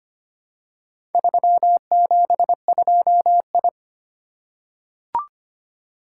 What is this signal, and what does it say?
Morse code, heard as a keyed tone, about 25 wpm, 700 Hz. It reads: 372I
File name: practice / 25wpm / Four Letter-Number Combinations - 25wpm